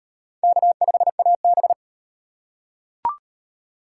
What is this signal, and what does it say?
Morse code, heard as a keyed tone, about 38 wpm, 700 Hz. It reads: K5AB